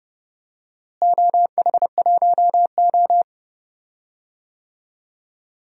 Morse code, heard as a keyed tone, about 30 wpm, 700 Hz. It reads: OH1O
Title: DXCC-100 Call Signs - 4 Characters - Repeated 2x - 30wpm